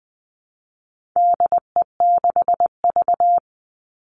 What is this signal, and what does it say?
Morse code, heard as a keyed tone, about 20 wpm, 700 Hz. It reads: DE6V
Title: All Cut Numbers - 4-Digit 20wpm